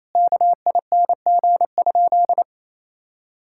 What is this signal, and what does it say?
Morse code, heard as a keyed tone, about 28 wpm, 700 Hz. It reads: KING?
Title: Sentences from Top 1000 Words - Review Part 5 - 28wpm